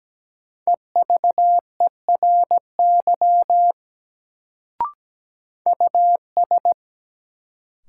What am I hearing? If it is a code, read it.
Morse code, heard as a keyed tone, about 17 wpm, 700 Hz. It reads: EVERY US
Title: Sentences from Top 200 Words - New Word Drill - 17wpm